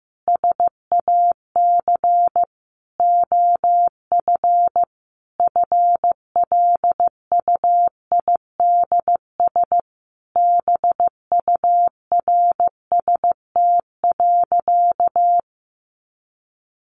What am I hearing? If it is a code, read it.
Morse code, heard as a keyed tone, about 15 wpm, 700 Hz. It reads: SAC OF FLUIDS BURST.
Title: A Sound Of Thunder 15wpm